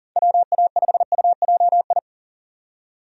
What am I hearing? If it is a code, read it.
Morse code, heard as a keyed tone, about 40 wpm, 700 Hz. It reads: WA5UJI
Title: USA Call Signs - 6 Characters - 40wpm